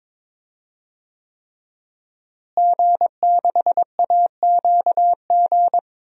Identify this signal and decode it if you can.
Morse code, heard as a keyed tone, about 21 wpm, 700 Hz. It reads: G6AQG
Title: DXCC-100 Call Signs - 5 Characters - Repeated 3x - 22wpm